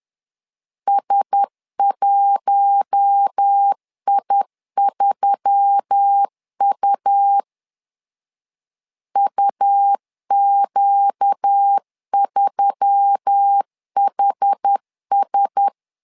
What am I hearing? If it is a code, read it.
Morse code, heard as a keyed tone, about 10 wpm, 800 Hz. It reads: S1I3U UQ3HS
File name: LEZIONI / 5 LBVC / 5 L 10wpm 2259